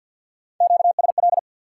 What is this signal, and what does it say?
Morse code, heard as a keyed tone, about 54 wpm, 700 Hz. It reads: QSL
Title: Top 180 QSO Elements 50wpm